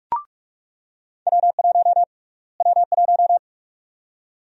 Morse code, heard as a keyed tone, about 46 wpm, 700 Hz. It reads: W1 W1
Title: USA Call Sign Prefixes - Repeated 2x - 45wpm